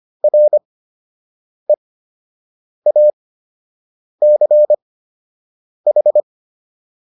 Morse code, heard as a keyed tone, about 25 wpm, spaced as for 7 wpm, 600 Hz. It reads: REACH